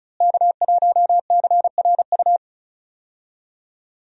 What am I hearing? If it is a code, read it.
Morse code, heard as a keyed tone, about 36 wpm, 700 Hz. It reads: K1CRU
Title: USA Call Signs - 5 Characters - Repeated 3x - 35wpm